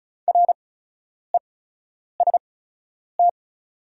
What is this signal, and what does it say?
Morse code, heard as a keyed tone, about 36 wpm, spaced as for 10 wpm, 690 Hz. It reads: REST
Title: 8x Character Spacing - Top 500 Words - 35wpm